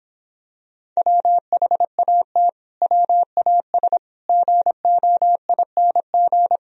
Morse code, heard as a keyed tone, about 26 wpm, 700 Hz. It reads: WHAT WAS GOING